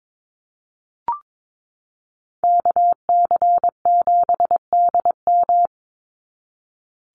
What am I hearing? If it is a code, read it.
Morse code, heard as a keyed tone, about 22 wpm, 700 Hz. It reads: KC7DM
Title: NA SOTA Activator Callsigns 22wpm